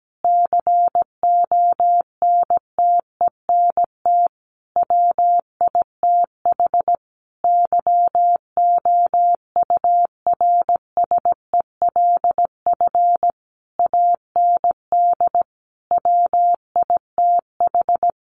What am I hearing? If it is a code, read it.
Morse code, heard as a keyed tone, about 17 wpm, 700 Hz. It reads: CONTENT WITH YOURSELF AND WITH
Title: Sentences from Top 900 Words 17wpm